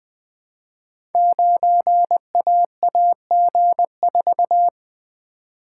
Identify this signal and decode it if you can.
Morse code, heard as a keyed tone, about 20 wpm, 700 Hz. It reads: 9AAG4
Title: Code Groups - 5 Letter:Numbers - 20wpm